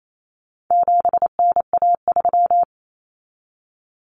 Morse code, heard as a keyed tone, about 28 wpm, 700 Hz. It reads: 7NA3
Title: Common Cut Numbers - 4-Digit 28wpm